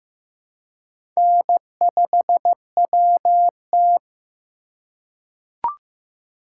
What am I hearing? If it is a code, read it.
Morse code, heard as a keyed tone, about 15 wpm, 700 Hz. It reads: N5WT
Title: USA Call Signs - 4 Characters - 2x Character Spacing - 15wpm